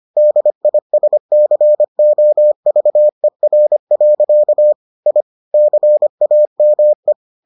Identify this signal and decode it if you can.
Morse code, heard as a keyed tone, about 25 wpm, 600 Hz. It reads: DISCOVER. I CAME